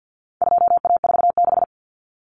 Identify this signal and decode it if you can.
Morse code, heard as a keyed tone, about 48 wpm, 700 Hz. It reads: 2A46
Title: Common Cut Numbers - 4-Digit 50wpm